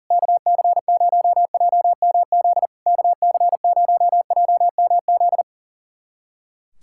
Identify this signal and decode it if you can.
Morse code, heard as a keyed tone, about 44 wpm, 700 Hz. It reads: KC0JMZ KC0JMZ
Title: SOTA QSO Practice 40wpm